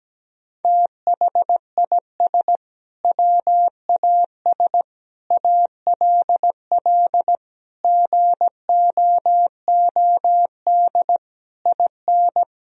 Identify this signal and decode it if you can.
Morse code, heard as a keyed tone, about 17 wpm, 700 Hz. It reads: THIS WAS ALL GOOD IN
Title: Sentences from Top 1000 Words - Review Part 3 - 17wpm